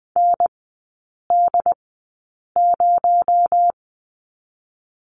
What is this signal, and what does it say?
Morse code, heard as a keyed tone, about 19 wpm, spaced as for 8 wpm, 700 Hz. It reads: ND0
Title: 218 Lesson 35e - Number 0 - Three Characters 20wpm